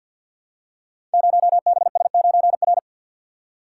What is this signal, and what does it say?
Morse code, heard as a keyed tone, about 52 wpm, 700 Hz. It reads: 0BS9R
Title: Code Groups - 5 Letter:Numbers - 50wpm